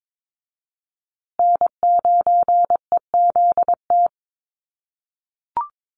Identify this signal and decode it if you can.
Morse code, heard as a keyed tone, about 22 wpm, 700 Hz. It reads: N9EZT
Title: USA Call Signs Repeated 3x 22wpm